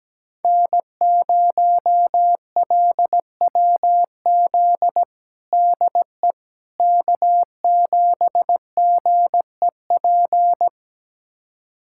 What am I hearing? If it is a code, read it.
Morse code, heard as a keyed tone, about 17 wpm, 700 Hz. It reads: N0LWZ DE K7GEP
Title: USA Callsign DE Callsign - 5 Characters - 17wpm